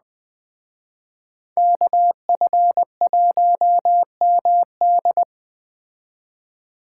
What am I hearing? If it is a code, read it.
Morse code, heard as a keyed tone, about 20 wpm, 700 Hz. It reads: KF1MD
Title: Speed Racing - USA 2x2 Call Signs 17wpm